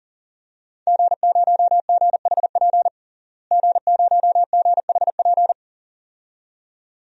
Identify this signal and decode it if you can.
Morse code, heard as a keyed tone, about 40 wpm, 700 Hz. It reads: G0GHP G0GHP
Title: DXCC-100 Call Signs - 5 Characters - Repeated 2x - 40wpm